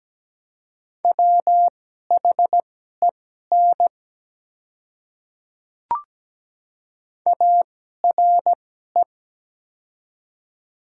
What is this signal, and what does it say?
Morse code, heard as a keyed tone, about 17 wpm, spaced as for 12 wpm, 700 Hz. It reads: WHEN ARE
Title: 2x Character Spacing - Top 100 Words - 17wpm